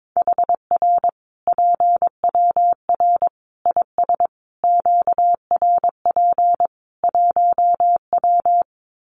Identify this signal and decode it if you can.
Morse code, heard as a keyed tone, about 22 wpm, 700 Hz. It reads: HR PWR IS QRP 1W
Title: Common QSO Phrases 22wpm